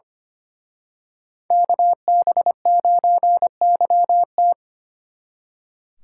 Morse code, heard as a keyed tone, about 29 wpm, 700 Hz. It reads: KB9YT
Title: Speed Racing - USA 2x2 Call Signs 25wpm